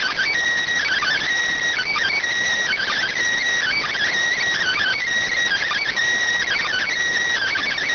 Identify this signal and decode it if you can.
Emergency Alert System (SAME) tones (header bursts), heard as harsh buzzing bursts. It is heard from 0.8 to 6.9 s.